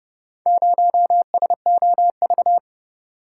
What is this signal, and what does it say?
Morse code, heard as a keyed tone, about 29 wpm, 700 Hz. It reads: 0SOV